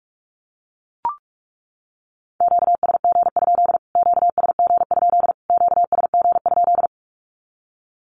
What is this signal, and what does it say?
Morse code, heard as a keyed tone, about 44 wpm, 700 Hz. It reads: QSG? QSG? QSG?
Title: Q-Codes Repeated 3x 45wpm